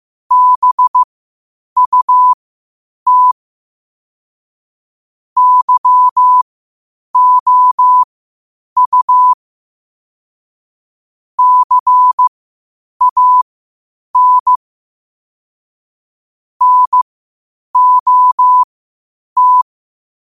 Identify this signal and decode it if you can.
Morse code, heard as a keyed tone, about 15 wpm, spaced as for 8 wpm, 1000 Hz. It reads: BUT YOU CAN NOT